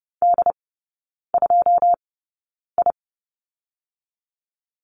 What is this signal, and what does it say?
Morse code, heard as a keyed tone, about 30 wpm, spaced as for 9 wpm, 700 Hz. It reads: D2I
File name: 081 Lesson 14e - Number 2 - Three Characters 30wpm